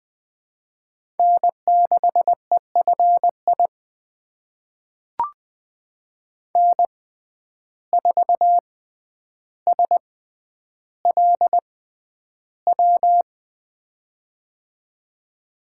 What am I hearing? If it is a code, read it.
Morse code, heard as a keyed tone, about 20 wpm, spaced as for 10 wpm, 700 Hz. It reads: N6EFI N4SLW
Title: USA Call Signs - 5 Characters - 6x Character Spacing - 20wpm